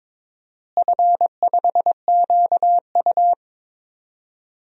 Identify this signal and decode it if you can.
Morse code, heard as a keyed tone, about 22 wpm, 700 Hz. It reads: F5QU
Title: DXCC-100 Call Signs - 4 Characters - Repeated 3x - 22wpm